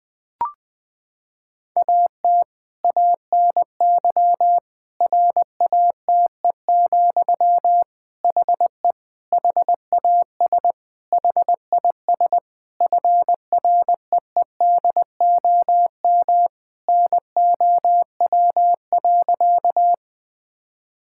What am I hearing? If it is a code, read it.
Morse code, heard as a keyed tone, about 20 wpm, 700 Hz. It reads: AT ANY RATE, HE HAS HIS FREEDOM NOW.
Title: Sentences from Top 1000 Words - Review Part 5 - 20wpm